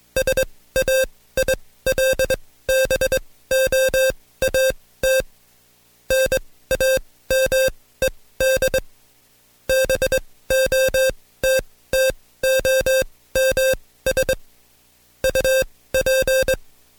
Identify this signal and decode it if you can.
Morse code, heard as a keyed tone, about 22 wpm, spaced as for 16 wpm, 530 Hz. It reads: SAILBOAT NAMED BOTTOMS UP
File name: Code Quick / AudioDownld2 / Seg134-Stepper7 15WPM